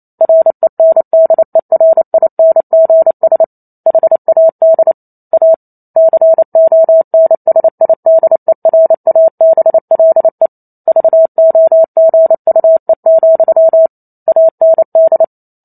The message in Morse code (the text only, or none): RENDERINGS HAD A CONSIDERABLE VOGUE, AND